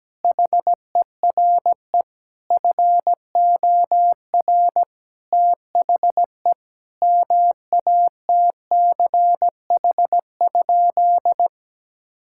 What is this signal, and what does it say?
Morse code, heard as a keyed tone, about 17 wpm, 700 Hz. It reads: HERE FOR THE MATCH?
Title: Sentences from Top 1600 Words 17wpm